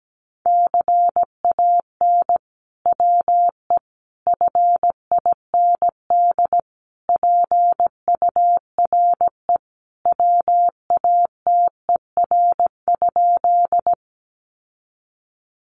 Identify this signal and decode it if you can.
Morse code, heard as a keyed tone, about 17 wpm, 700 Hz. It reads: CAN WE FIND PURE WATER?
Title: Sentences from Top 1500 Words 17wpm